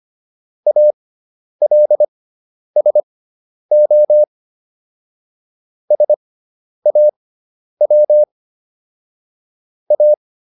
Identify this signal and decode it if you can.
Morse code, heard as a keyed tone, about 25 wpm, spaced as for 10 wpm, 600 Hz. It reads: ALSO SAW A